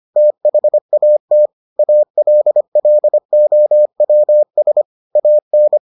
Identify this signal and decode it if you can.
Morse code, heard as a keyed tone, about 25 wpm, 600 Hz. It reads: THAT ALLOWS AN